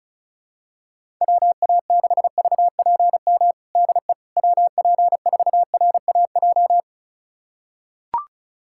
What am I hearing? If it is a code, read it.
Morse code, heard as a keyed tone, about 36 wpm, 700 Hz. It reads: WA6VPM DE WP4RAJ
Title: USA Callsign DE Callsign - 6 Characters - 35wpm